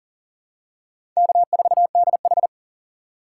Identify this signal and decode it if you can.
Morse code, heard as a keyed tone, about 40 wpm, 700 Hz. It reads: K4DH
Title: USA Call Signs - 4 Characters - 4x Character Spacing - 40wpm